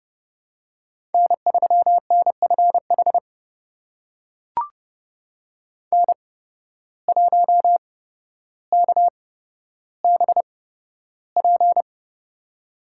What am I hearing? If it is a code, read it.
Morse code, heard as a keyed tone, about 30 wpm, spaced as for 13 wpm, 700 Hz. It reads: N3NFH N1KBP
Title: USA Call Signs - 5 Characters - 8x Character Spacing - 30wpm